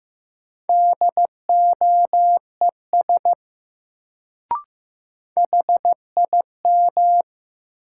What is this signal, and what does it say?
Morse code, heard as a keyed tone, about 15 wpm, 700 Hz. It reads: DOES HIM
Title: Sets of 2 Words - New Word Drill - 15wpm